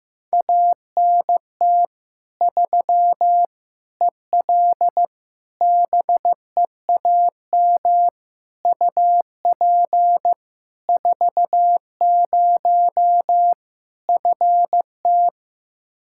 Morse code, heard as a keyed tone, about 15 wpm, 700 Hz. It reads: ANT 3 EL BEAM UP 40 FT
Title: K1USN SST QSO Practice 15wpm